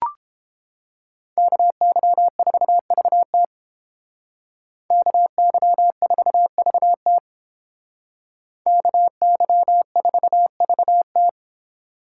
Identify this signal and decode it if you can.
Morse code, heard as a keyed tone, about 29 wpm, 700 Hz. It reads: KY4VT KY4VT KY4VT